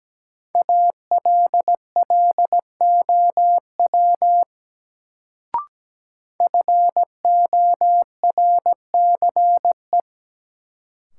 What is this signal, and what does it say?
Morse code, heard as a keyed tone, about 17 wpm, 700 Hz. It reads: ALLOW FORCE
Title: Sentences from Top 700 Words - New Word Drill - 17wpm